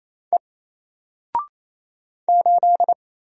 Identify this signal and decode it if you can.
Morse code, heard as a keyed tone, about 28 wpm, 700 Hz. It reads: E 8
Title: Single Letter-Number Alt 28wpm